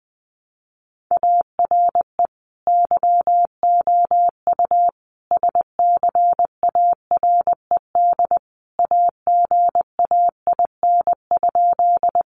Morse code, heard as a keyed tone, about 20 wpm, 700 Hz. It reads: ARE YOU SCARED AGAIN?
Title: Sentences from Top 700 Words 20wpm